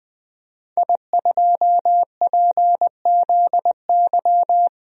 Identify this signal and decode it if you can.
Morse code, heard as a keyed tone, about 20 wpm, 700 Hz. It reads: I2PZY
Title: DXCC-100 Call Signs - 5 Characters - 20wpm